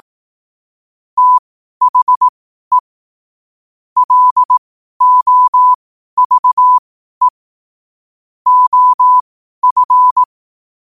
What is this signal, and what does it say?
Morse code, heard as a keyed tone, about 18 wpm, spaced as for 12 wpm, 1000 Hz. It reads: THE LOVE OF
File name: qotdfiles / 12wpm